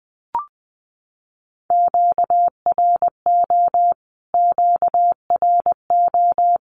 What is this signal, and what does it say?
Morse code, heard as a keyed tone, about 19 wpm, 700 Hz. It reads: QRO QRO
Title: Q-Codes Repeated 3x 20wpm